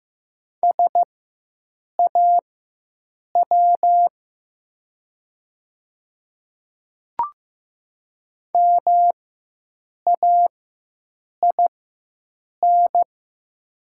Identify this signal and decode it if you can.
Morse code, heard as a keyed tone, about 15 wpm, spaced as for 7 wpm, 700 Hz. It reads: SAW MAIN
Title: 4x Character Spacing - Top 300 Words - 15wpm